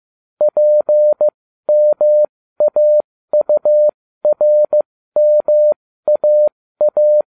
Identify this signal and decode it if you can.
Morse code, heard as a keyed tone, about 15 wpm, spaced as for 13 wpm, 600 Hz. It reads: PMAURMAA